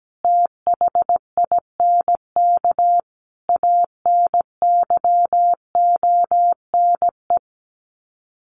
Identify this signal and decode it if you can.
Morse code, heard as a keyed tone, about 17 wpm, 700 Hz. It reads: THINK ANYONE
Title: Sets of 5 Words Encore 17wpm